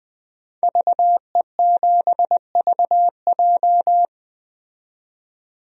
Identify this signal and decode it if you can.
Morse code, heard as a keyed tone, about 20 wpm, 700 Hz. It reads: VE7VJ
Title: DXCC-100 Call Signs - 5 Characters - 20wpm